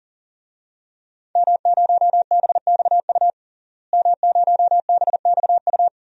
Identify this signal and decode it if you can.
Morse code, heard as a keyed tone, about 40 wpm, 700 Hz. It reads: M0BXU M0BXU
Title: DXCC-100 Call Signs - 5 Characters - Repeated 3x - 40wpm